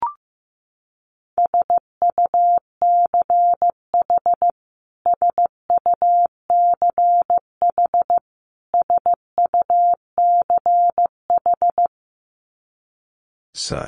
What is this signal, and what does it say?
Morse code, heard as a keyed tone, about 15 wpm, 700 Hz. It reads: SUCH SUCH SUCH
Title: Top 200 Words Repeated 15wpm